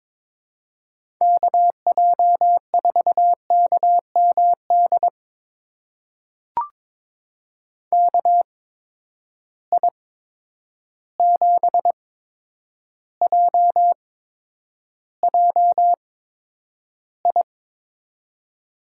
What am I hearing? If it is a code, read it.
Morse code, heard as a keyed tone, about 22 wpm, spaced as for 9 wpm, 700 Hz. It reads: KJ4KMD KI7JJI
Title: USA Call Signs - 6 Characters - 8x Character Spacing - 22wpm